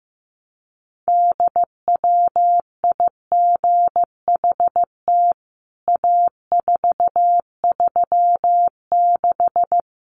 Morse code, heard as a keyed tone, about 15 wpm, 700 Hz. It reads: DWIGHT A436
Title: CWT Exchange 15wpm